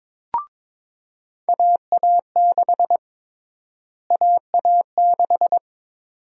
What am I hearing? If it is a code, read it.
Morse code, heard as a keyed tone, about 22 wpm, 700 Hz. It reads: AA6 AA6
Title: USA Call Sign Prefixes - Repeated 2x - 22wpm